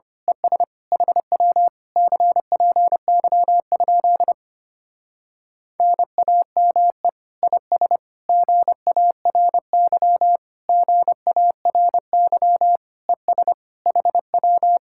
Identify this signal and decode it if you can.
Morse code, heard as a keyed tone, about 26 wpm, 700 Hz. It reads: ES HW CPY? NAME IS GARY GARY ES HW
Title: Speed-Racing Common QSO Phrases Encore - 25wpm